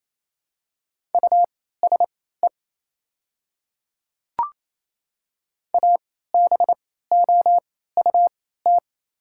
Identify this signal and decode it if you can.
Morse code, heard as a keyed tone, about 28 wpm, spaced as for 16 wpm, 700 Hz. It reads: USE ABOUT
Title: 3x Character Spacing - Top 100 Words - 28wpm